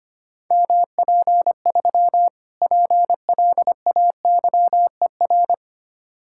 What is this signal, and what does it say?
Morse code, heard as a keyed tone, about 25 wpm, 700 Hz. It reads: MP3 PLAYER
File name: practice / 25wpm / Difficult Words - Speed-Racing with Hints - 25wpm